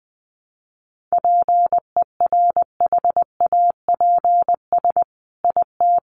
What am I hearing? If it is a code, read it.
Morse code, heard as a keyed tone, about 20 wpm, 700 Hz. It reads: PERHAPS IT
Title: Sentences from Top 1500 Words 20wpm